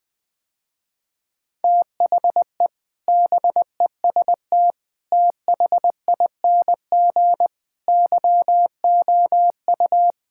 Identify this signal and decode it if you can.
Morse code, heard as a keyed tone, about 20 wpm, 700 Hz. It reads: THE BEST THING YOU
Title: Sentences with 7-Word Phrases - Spoken Beginning - 20wpm